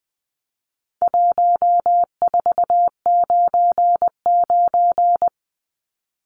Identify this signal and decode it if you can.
Morse code, heard as a keyed tone, about 20 wpm, 700 Hz. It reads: 1499
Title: Numbers - 4-Digit 20wpm